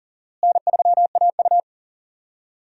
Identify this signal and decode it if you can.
Morse code, heard as a keyed tone, about 40 wpm, 700 Hz. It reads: N3AU
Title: USA Call Signs - 4 Characters - Repeated 3x - 40wpm